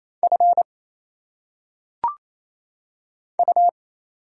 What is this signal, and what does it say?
Morse code, heard as a keyed tone, about 27 wpm, 700 Hz. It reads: F U
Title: Single Letters - Phonetic Pronunciation 28wpm